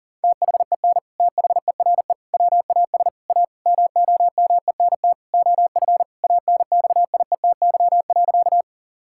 Morse code, heard as a keyed tone, about 40 wpm, 700 Hz. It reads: THEN THERE WAS A MOMENT OF ANXIETY.
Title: Sentences from Top 1600 Words 40wpm